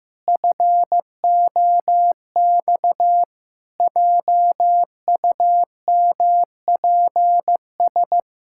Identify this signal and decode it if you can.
Morse code, heard as a keyed tone, about 16 wpm, 700 Hz. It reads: FOX JUMPS
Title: Pangram Challenge v2 - 15wpm